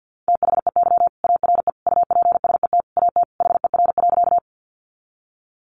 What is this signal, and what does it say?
Morse code, heard as a keyed tone, about 50 wpm, 700 Hz. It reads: THEY ARE UPSET AT HER.
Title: Sentences from Top 800 Words 50wpm